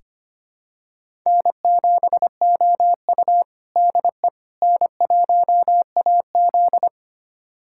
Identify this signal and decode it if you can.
Morse code, heard as a keyed tone, about 25 wpm, 700 Hz. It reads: N7OU DE N1AZ
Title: USA Callsign DE Callsign - 4 Characters - 25wpm